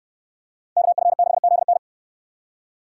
Morse code, heard as a keyed tone, about 64 wpm, 700 Hz. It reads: KF6ZN